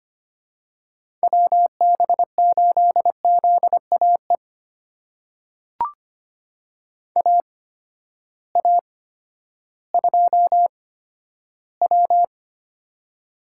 Morse code, heard as a keyed tone, about 25 wpm, spaced as for 12 wpm, 700 Hz. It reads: WB8ZAE AA2W
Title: USA Call Signs - 8x Character Spacing - 25wpm